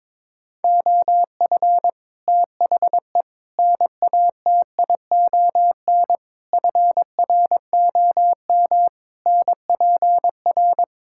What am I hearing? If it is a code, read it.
Morse code, heard as a keyed tone, about 22 wpm, 700 Hz. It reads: OF THE NATION FROM NPR
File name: practice / 22wpm / Build A Phrase - 5-Word Phrases - 22wpm